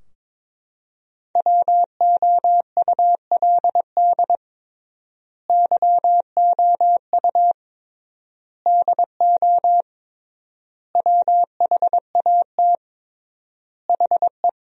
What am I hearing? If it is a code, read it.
Morse code, heard as a keyed tone, about 22 wpm, 700 Hz. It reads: WOULD YOU DO WHAT HE
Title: 3x Word-Spacing - Sentences from Top 100 Words - 22wpm